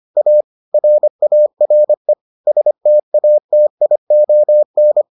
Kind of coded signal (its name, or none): Morse code